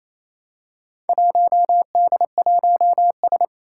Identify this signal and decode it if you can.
Morse code, heard as a keyed tone, about 28 wpm, 700 Hz. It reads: 1D1S
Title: Four Challenging Letter-Number Combinations 28wpm